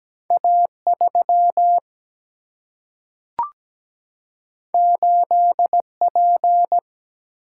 Morse code, heard as a keyed tone, about 17 wpm, 700 Hz. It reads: A 3 8 P
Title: Single-Letter Single-Number 17wpm